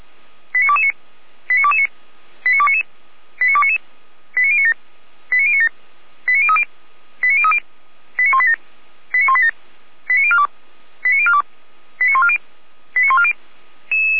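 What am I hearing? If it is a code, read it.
Five-tone selective call (ZVEI), heard as quick runs of five tones, tones of about 80 ms. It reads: 89229 89229 89299 89299 89097 89097 89030 89030 89187 89187 89042 89042 89140 89140
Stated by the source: ZVEI five-tone call 89277 received on an SDR, from the feuerwehr-orlamuende.org fire brigade audio server.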